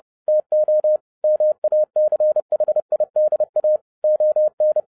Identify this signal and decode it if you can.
Morse code, heard as a keyed tone, about 30 wpm, 610 Hz. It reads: TO MACHIDA ON